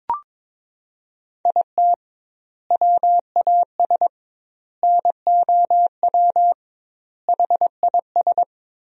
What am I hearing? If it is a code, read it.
Morse code, heard as a keyed tone, about 22 wpm, 700 Hz. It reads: IT WAS NOW HIS
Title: 2x Word-Spacing - Sentences from Top 200 Words - 22wpm